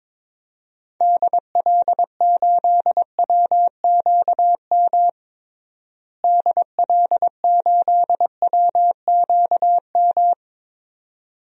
Morse code, heard as a keyed tone, about 22 wpm, 700 Hz. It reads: DL8WQM DL8WQM
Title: DXCC-100 Call Signs - 6 Characters - Repeated 2x - 22wpm